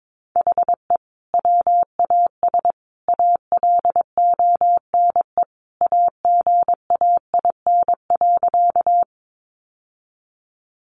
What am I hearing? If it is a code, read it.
Morse code, heard as a keyed tone, about 22 wpm, 700 Hz. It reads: HE WAS ALONE AGAIN.
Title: Sentences from Top 1000 Words 22wpm